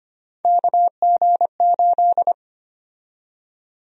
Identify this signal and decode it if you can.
Morse code, heard as a keyed tone, about 25 wpm, 700 Hz. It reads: KG8
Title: USA Call Sign Prefixes - 25wpm